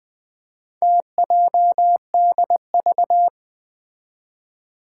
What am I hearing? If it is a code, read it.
Morse code, heard as a keyed tone, about 20 wpm, 700 Hz. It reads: TJDV
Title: Four Challenging Letter-Number Combinations 20wpm